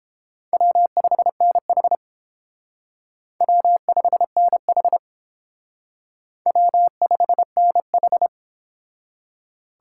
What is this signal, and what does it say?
Morse code, heard as a keyed tone, about 29 wpm, 700 Hz. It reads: W5NH W5NH W5NH